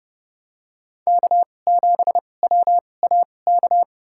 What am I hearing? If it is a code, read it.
Morse code, heard as a keyed tone, about 30 wpm, spaced as for 22 wpm, 700 Hz. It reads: K7WAK
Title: USA Call Signs - 5 Characters - 2x Character Spacing - 30wpm